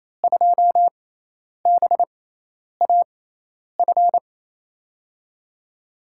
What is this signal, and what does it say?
Morse code, heard as a keyed tone, about 28 wpm, spaced as for 10 wpm, 700 Hz. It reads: 2BAF
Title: USA Call Signs - 5 Characters - 6x Character Spacing - 28wpm